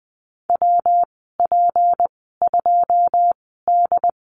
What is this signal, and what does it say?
Morse code, heard as a keyed tone, about 20 wpm, spaced as for 15 wpm, 700 Hz. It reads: WP2D
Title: 173 Lesson 28g - Number 9 - Call Signs 20wpm